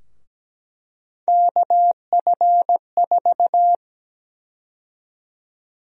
Morse code, heard as a keyed tone, about 17 wpm, 700 Hz. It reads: KF4
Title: USA Call Sign Prefixes - Repeated 2x - 17wpm